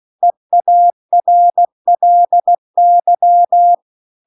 Morse code, heard as a keyed tone, about 16 wpm, 700 Hz. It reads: EARLY
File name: CW-The-Big-Twoheaded-River-2-16WPM